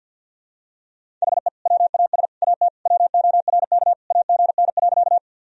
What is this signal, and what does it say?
Morse code, heard as a keyed tone, about 49 wpm, 700 Hz. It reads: HE WAS AT WORK AGN.
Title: Sentences from Top 200 Words and Abbreviations 50wpm